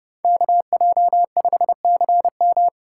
Morse code, heard as a keyed tone, about 29 wpm, 700 Hz. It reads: KJ5CM